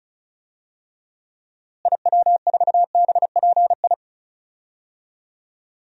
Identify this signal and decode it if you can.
Morse code, heard as a keyed tone, about 36 wpm, 700 Hz. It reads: IW4BPI
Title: DXCC-100 Call Signs - 6 Characters - Repeated 2x - 35wpm